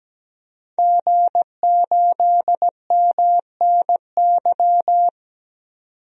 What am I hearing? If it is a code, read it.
Morse code, heard as a keyed tone, about 17 wpm, 700 Hz. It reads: G8MNY
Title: DXCC-100 Call Signs - 5 Characters - 17wpm